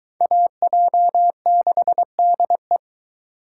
Morse code, heard as a keyed tone, about 22 wpm, 700 Hz. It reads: AJ6DE